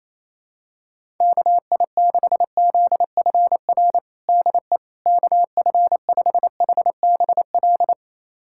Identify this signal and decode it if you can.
Morse code, heard as a keyed tone, about 28 wpm, 700 Hz. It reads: KI6ZFR DE KF5HBL
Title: USA Callsign DE Callsign - 6 Characters - 28wpm